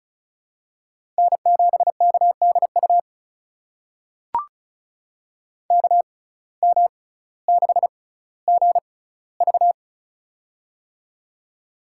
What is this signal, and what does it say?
Morse code, heard as a keyed tone, about 34 wpm, spaced as for 18 wpm, 700 Hz. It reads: N7KDU KM6GV
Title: USA Call Signs - 5 Characters - 6x Character Spacing - 35wpm